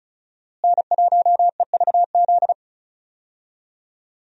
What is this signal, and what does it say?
Morse code, heard as a keyed tone, about 36 wpm, 700 Hz. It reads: N1EVZ
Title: USA Call Signs - 5 Characters - Repeated 3x - 35wpm